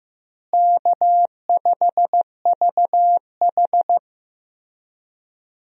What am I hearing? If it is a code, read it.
Morse code, heard as a keyed tone, about 15 wpm, 700 Hz. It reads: K5VH
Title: USA Call Signs - 4 Characters - Repeated 3x - 15wpm